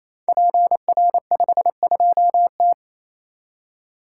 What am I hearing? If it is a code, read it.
Morse code, heard as a keyed tone, about 29 wpm, 700 Hz. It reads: PR52T
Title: Code Groups - 5 Letter:Numbers - 28wpm